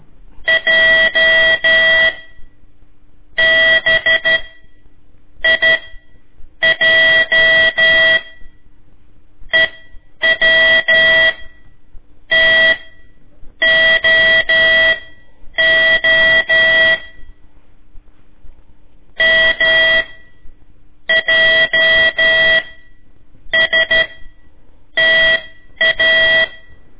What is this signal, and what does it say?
Morse code, heard as a keyed tone, about 13 wpm, spaced as for 7 wpm, 1950 Hz. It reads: J BIJ EWTOO MJSTA
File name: ik1weg2 / CORSI CW / lez4-1